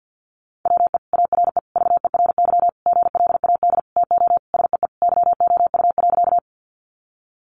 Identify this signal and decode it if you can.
Morse code, heard as a keyed tone, about 50 wpm, 700 Hz. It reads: WE ARE VERY GLAD TO SEE YOU.
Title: Sentences from Top 800 Words 50wpm